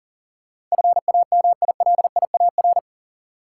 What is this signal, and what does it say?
Morse code, heard as a keyed tone, about 40 wpm, 700 Hz. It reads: FAMILIAR
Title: Sentences from Top 1000 Words - New Word Drill - 40wpm